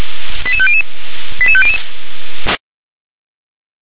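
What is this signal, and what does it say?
Five-tone selective call (ZVEI), heard as quick runs of five tones, tones of about 70 ms. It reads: 88500 88500
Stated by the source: ZVEI five-tone selective call, received on an SDR.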